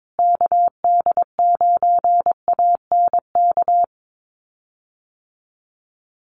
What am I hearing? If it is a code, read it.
Morse code, heard as a keyed tone, about 22 wpm, 700 Hz. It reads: KD9ANK
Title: USA Call Signs Repeated 3x 22wpm